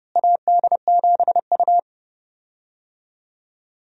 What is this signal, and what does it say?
Morse code, heard as a keyed tone, about 29 wpm, 700 Hz. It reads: AD7U
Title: USA Call Signs - 4 Characters - Repeated 3x - 30wpm